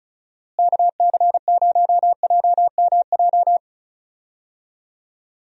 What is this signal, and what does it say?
Morse code, heard as a keyed tone, about 36 wpm, 700 Hz. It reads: KC0JMJ
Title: USA Call Signs - 6 Characters - Repeated 3x - 35wpm